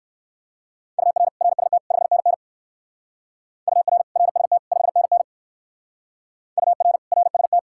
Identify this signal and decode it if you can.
Morse code, heard as a keyed tone, about 56 wpm, 700 Hz. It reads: UR RST 5NN UR RST 5NN UR RST
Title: Speed-Racing Common QSO Phrases - 45wpm